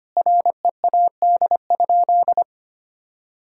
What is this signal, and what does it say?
Morse code, heard as a keyed tone, about 25 wpm, 700 Hz. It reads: READ?
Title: Sentences from Top 1000 Words - Review Part 5 - 25wpm